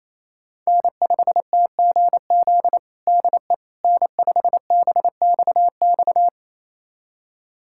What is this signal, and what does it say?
Morse code, heard as a keyed tone, about 28 wpm, 700 Hz. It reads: N5TGZ DE N5BXX
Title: USA Callsign DE Callsign - 5 Characters - 28wpm